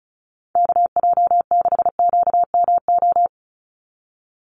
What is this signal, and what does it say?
Morse code, heard as a keyed tone, about 34 wpm, 700 Hz. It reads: KJ6QMO
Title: USA Call Signs 35wpm